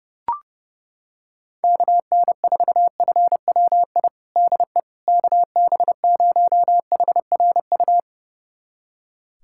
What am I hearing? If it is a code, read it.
Morse code, heard as a keyed tone, about 30 wpm, 700 Hz. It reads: KN4FWI DE KB0HRU
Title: USA Callsign DE Callsign - 6 Characters - 30wpm